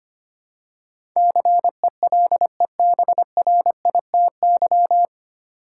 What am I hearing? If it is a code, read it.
Morse code, heard as a keyed tone, about 25 wpm, 700 Hz. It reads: CELEBRITY
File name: Nine Letter Words - 25wpm